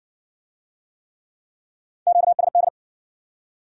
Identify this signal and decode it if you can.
Morse code, heard as a keyed tone, about 55 wpm, 700 Hz. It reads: QSD